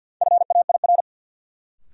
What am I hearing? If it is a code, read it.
Morse code, heard as a keyed tone, about 47 wpm, 700 Hz. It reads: FAIR
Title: Sentences from Top 1300 Words - New Word Drill Alt - 50wpm